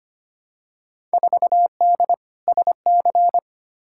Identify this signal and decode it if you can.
Morse code, heard as a keyed tone, about 25 wpm, 700 Hz. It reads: 4D SC
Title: Field Day Exchanges Encore - 25wpm